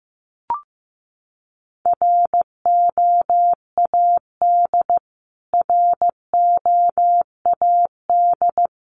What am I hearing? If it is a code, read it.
Morse code, heard as a keyed tone, about 15 wpm, 700 Hz. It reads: ROAD ROAD
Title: Top 500 Words Repeated 15wpm